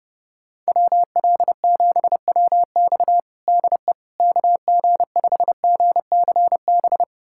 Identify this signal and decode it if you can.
Morse code, heard as a keyed tone, about 30 wpm, 700 Hz. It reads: WL7WX DE KG5GCB
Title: USA Callsign DE Callsign 30wpm